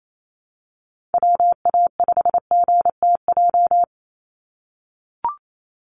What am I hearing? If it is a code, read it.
Morse code, heard as a keyed tone, about 28 wpm, 700 Hz. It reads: WA5GTJ
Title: USA Call Signs 28wpm